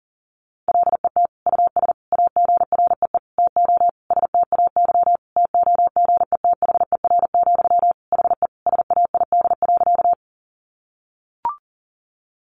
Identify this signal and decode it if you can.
Morse code, heard as a keyed tone, about 37 wpm, 700 Hz. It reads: LET US AGREE TO STAY TOGETHER, HE SAID.
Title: Sentences from Top 1100 Words 40wpm